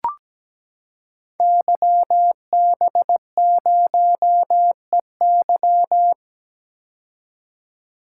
Morse code, heard as a keyed tone, about 17 wpm, 700 Hz. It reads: YB0EY
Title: Code Groups - 5 Letter:Numbers - 17wpm